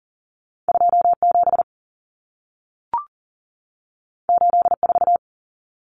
Morse code, heard as a keyed tone, about 40 wpm, 700 Hz. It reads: 27 84
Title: Numbers - 2-Digit 40wpm